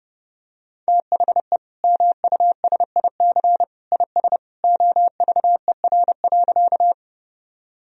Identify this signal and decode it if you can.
Morse code, heard as a keyed tone, about 30 wpm, 700 Hz. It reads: THE MUSIC IS OVER.
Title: Sentences from Top 1000 Words - Review Part 1 - 30wpm